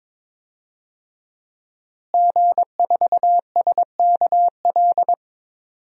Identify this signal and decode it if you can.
Morse code, heard as a keyed tone, about 22 wpm, 700 Hz. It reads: G4SKL
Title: DXCC-100 Call Signs - 5 Characters - Repeated 3x - 22wpm